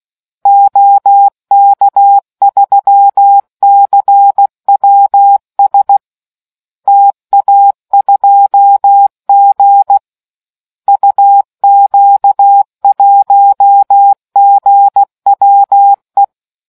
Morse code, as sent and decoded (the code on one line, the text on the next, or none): --- -.- ...-- -.-. .-- ... / - .- ..--- --. / ..- --.- .---- --. .-- .
OK3CWS TA2G UQ1GWE